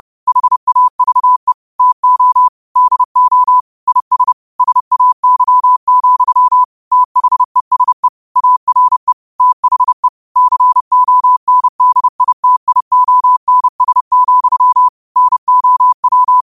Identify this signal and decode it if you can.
Morse code, heard as a keyed tone, about 30 wpm, 1000 Hz. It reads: HAVE TO DO IS SAY, THESE ARE THE CONDITIONS, NOW
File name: qotdfiles / 30wpm